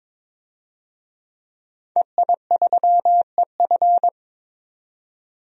Morse code, heard as a keyed tone, about 22 wpm, 700 Hz. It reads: EI3EF
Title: DXCC-100 Call Signs - 5 Characters - 22wpm